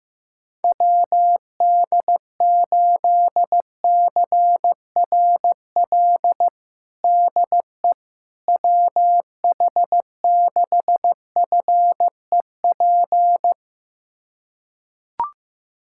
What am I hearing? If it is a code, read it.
Morse code, heard as a keyed tone, about 15 wpm, 700 Hz. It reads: WD8CRL DE WH6FEP
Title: USA Callsign DE Callsign - 6 Characters - 15wpm